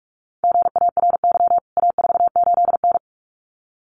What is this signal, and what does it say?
Morse code, heard as a keyed tone, about 44 wpm, 700 Hz. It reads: GARY A48N
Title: CWT Exchange 45wpm